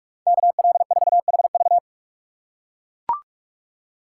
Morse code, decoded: KL4HV

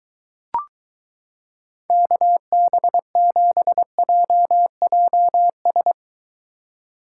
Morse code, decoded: KB7JJS

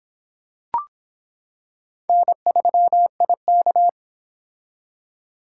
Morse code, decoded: N3IK

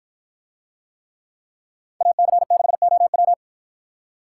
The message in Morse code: AC6OW

